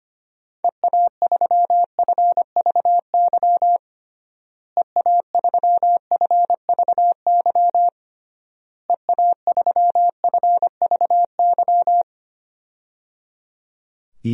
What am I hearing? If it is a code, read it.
Morse code, heard as a keyed tone, about 25 wpm, 700 Hz. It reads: EA3FVY EA3FVY EA3FVY
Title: DXCC-100 Call Signs - 6 Characters - Repeated 3x - 25wpm